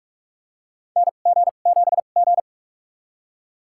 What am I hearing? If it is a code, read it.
Morse code, heard as a keyed tone, about 47 wpm, spaced as for 33 wpm, 700 Hz. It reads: NG7G
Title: USA Call Signs - 4 Characters - 2x Character Spacing - 45wpm